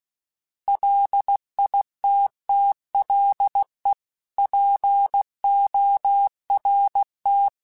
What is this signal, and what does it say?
Morse code, heard as a keyed tone, about 16 wpm, 800 Hz. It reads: LITTLEPORT
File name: fromSMs / FromSMs mp3s / Cities@15